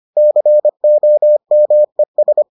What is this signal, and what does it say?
Morse code, heard as a keyed tone, about 25 wpm, 600 Hz. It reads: COMES